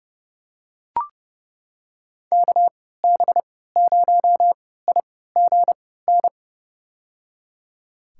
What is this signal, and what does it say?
Morse code, heard as a keyed tone, about 29 wpm, spaced as for 17 wpm, 700 Hz. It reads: KB0IGN
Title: USA Call Signs - 3x Character Spacing - 30wpm